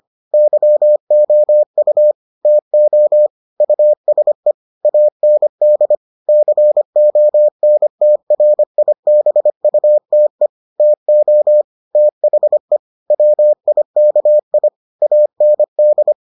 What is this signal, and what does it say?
Morse code, heard as a keyed tone, about 25 wpm, 600 Hz. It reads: YOU TO USE AND CONTRIBUTE TO THE WIKI AND